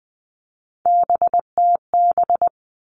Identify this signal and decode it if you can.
Morse code, heard as a keyed tone, about 20 wpm, 700 Hz. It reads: BTB
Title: All Cut Numbers - 3-Digit 20wpm